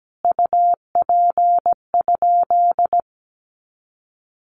Morse code, heard as a keyed tone, about 17 wpm, 700 Hz. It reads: UP?
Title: Sentences from Top 500 Words 17wpm